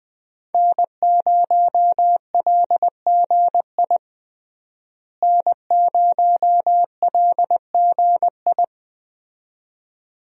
Morse code, heard as a keyed tone, about 20 wpm, 700 Hz. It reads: N0LGI N0LGI
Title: USA Call Signs - 5 Characters - Repeated 2x - 20wpm